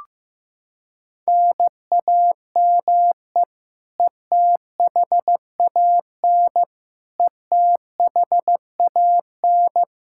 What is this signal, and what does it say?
Morse code, heard as a keyed tone, about 15 wpm, 700 Hz. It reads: NAME ETHAN ETHAN
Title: Practice QSO - Exchange 1 - Beginning - 15wpm